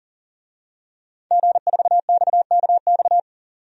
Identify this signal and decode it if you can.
Morse code, heard as a keyed tone, about 40 wpm, 700 Hz. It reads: G4XKX
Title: DXCC-100 Call Signs - 5 Characters - 40wpm